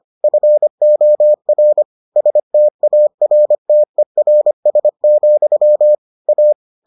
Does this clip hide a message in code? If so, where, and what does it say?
Yes: Morse code from 0.2 to 6.5 s, heard as a keyed tone, about 25 wpm, 600 Hz. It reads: FOR STARTERS, A